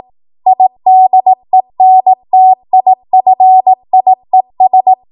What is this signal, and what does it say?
Morse code, heard as a keyed tone, about 18 wpm, 750 Hz. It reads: IDENTIFIES
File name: Morse / 18WPM